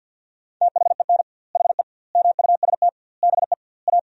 Morse code, heard as a keyed tone, about 51 wpm, 700 Hz. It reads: THEN HE MUST BE A
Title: 2x Word-Spacing - Sentences from Top 500 Words - 50wpm